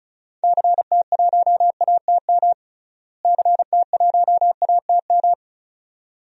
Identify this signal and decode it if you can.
Morse code, heard as a keyed tone, about 36 wpm, 700 Hz. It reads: CT1ATM CT1ATM
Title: DXCC-100 Call Signs - 6 Characters - Repeated 2x - 35wpm